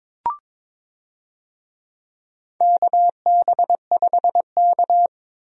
Morse code, heard as a keyed tone, about 22 wpm, 700 Hz. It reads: KB5K
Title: USA Call Signs - 4 Characters - Repeated 3x - 22wpm